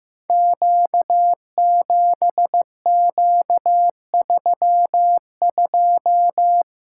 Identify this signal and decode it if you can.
Morse code, heard as a keyed tone, about 15 wpm, 700 Hz. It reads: Q7Q32
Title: Code Groups - 5 Letter:Numbers - 15wpm